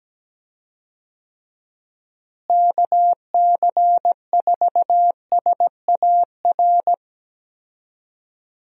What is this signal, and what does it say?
Morse code, heard as a keyed tone, about 17 wpm, 700 Hz. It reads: KC4SAR
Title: USA Call Signs - 6 Characters - Repeated 2x - 17wpm